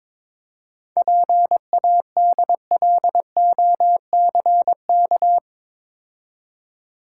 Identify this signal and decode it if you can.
Morse code, heard as a keyed tone, about 22 wpm, 700 Hz. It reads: PADLOCK
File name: Easy Words - Speed-Racing with Hints - 22wpm